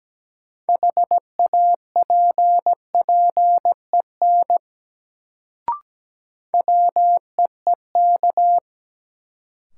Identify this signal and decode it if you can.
Morse code, heard as a keyed tone, about 17 wpm, 700 Hz. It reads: HAPPEN WEEK
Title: Sets of 5 Words Encore - New Word Drill - 17wpm